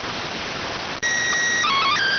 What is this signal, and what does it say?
SSTV (slow-scan television), PD 290.